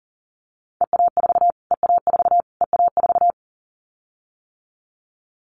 Morse code, heard as a keyed tone, about 40 wpm, 700 Hz. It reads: EA4 EA4 EA4
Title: Callsign Prefixes for Top 25 DXCC - Repeated 3x - 40wpm